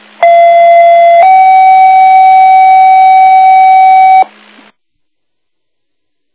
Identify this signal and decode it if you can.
Two-tone page, heard as a one-second tone, then a three-second tone, about 680 then 760 Hz. It is heard from 0.2 to 4.2 s.